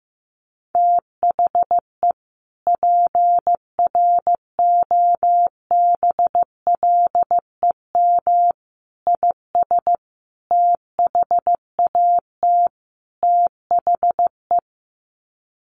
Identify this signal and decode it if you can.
Morse code, heard as a keyed tone, about 15 wpm, 700 Hz. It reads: THE PROBLEM IS THAT THE
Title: Sets of 5 Words Encore2 15wpm